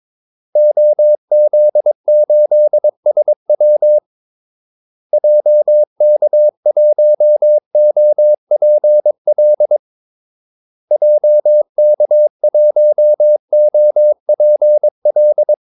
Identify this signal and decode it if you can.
Morse code, heard as a keyed tone, about 22 wpm, 600 Hz. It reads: OZ8SW JK1OPL JK1OPL
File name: Callsigns-22wpm